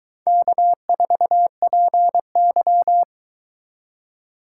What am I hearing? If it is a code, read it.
Morse code, heard as a keyed tone, about 22 wpm, 700 Hz. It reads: K4PY